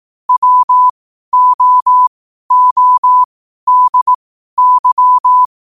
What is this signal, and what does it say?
Morse code, heard as a keyed tone, about 18 wpm, spaced as for 12 wpm, 1000 Hz. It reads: WOODY
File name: qotdfiles / 12wpm